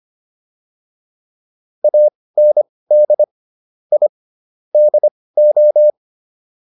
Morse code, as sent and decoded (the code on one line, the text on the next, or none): .- -. -.. / .. / -.. ---
AND I DO